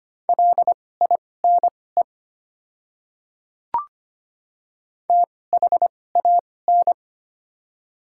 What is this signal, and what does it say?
Morse code, heard as a keyed tone, about 25 wpm, spaced as for 18 wpm, 700 Hz. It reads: LINE THAN
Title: 2x Character Spacing - Top 200 Words - 25wpm